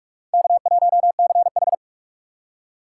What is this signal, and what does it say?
Morse code, heard as a keyed tone, about 47 wpm, 700 Hz. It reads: K1CH
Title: USA Call Signs - 4 Characters - 3x Character Spacing - 45wpm